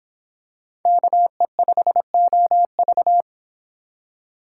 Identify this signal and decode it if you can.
Morse code, heard as a keyed tone, about 26 wpm, 700 Hz. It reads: KE5OV